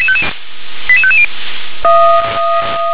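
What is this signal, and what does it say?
Five-tone selective call (ZVEI), heard as quick runs of five tones, tones of about 70 ms. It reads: 88550
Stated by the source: ZVEI five-tone call for address 88837, received by SDR.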